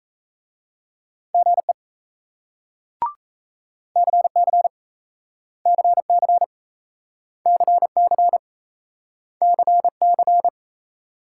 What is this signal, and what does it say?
Morse code, heard as a keyed tone, about 38 wpm, 700 Hz. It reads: GE CC CC CC CC